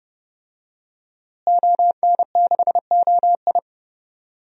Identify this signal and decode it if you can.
Morse code, heard as a keyed tone, about 30 wpm, 700 Hz. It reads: ON6OI
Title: DXCC-100 Call Signs - 5 Characters - Repeated 2x - 30wpm